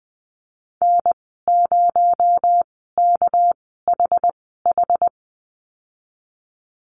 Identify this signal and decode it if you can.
Morse code, heard as a keyed tone, about 20 wpm, spaced as for 15 wpm, 700 Hz. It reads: N0KHH
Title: 231 Lesson 37g - Letter x - Call Signs 20wpm